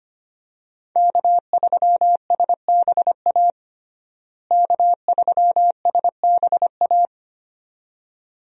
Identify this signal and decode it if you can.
Morse code, heard as a keyed tone, about 25 wpm, 690 Hz. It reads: K3SBA K3SBA
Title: USA Call Signs - 5 Characters - Repeated 2x - 25wpm